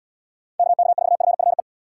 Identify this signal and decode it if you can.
Morse code, heard as a keyed tone, about 74 wpm, 700 Hz. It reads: KK4FFE